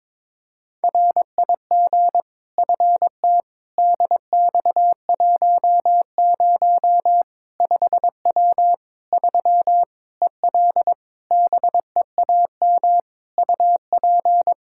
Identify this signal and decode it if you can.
Morse code, heard as a keyed tone, about 22 wpm, 700 Hz. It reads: RIG FT DX10 5W 3 EL BEAM UP
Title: Practice QSO - Exchange 2 - Info - 22wpm